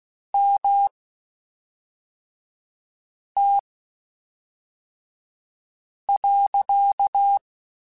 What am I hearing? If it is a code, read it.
Morse code, heard as a keyed tone, about 16 wpm, spaced as for 3 wpm, 800 Hz. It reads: MT.
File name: (spaced) / QSOs Spaced-mp3s / WD0BGP 1 @5 sp